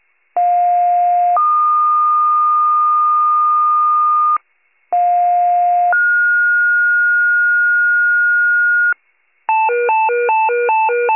Two-tone page, heard as a one-second tone, then a three-second tone, about 710 then 1220 Hz, about 710 then 1430 Hz, with noise. It is heard from 0.4 to 8.9 s.
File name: pager / 46 FIRE / wcalert